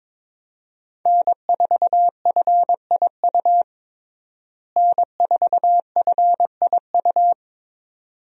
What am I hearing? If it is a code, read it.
Morse code, heard as a keyed tone, about 22 wpm, 700 Hz. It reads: N4FIU N4FIU
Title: USA Call Signs - 5 Characters - Repeated 2x - 22wpm